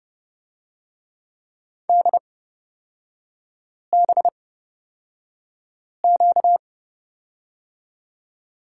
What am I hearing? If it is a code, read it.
Morse code, heard as a keyed tone, about 29 wpm, spaced as for 5 wpm, 700 Hz. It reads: DBQ
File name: Lesson 38e - Letter q - Three Characters 30wpm